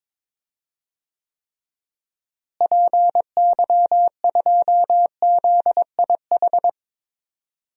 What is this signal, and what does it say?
Morse code, heard as a keyed tone, about 22 wpm, 700 Hz. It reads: PY2ZIH
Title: DXCC-100 Call Signs - 6 Characters - Repeated 3x - 22wpm